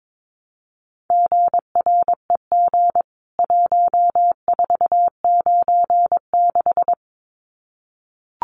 Morse code, heard as a keyed tone, about 22 wpm, 700 Hz. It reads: GREG 1496
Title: CWT Exchange 22wpm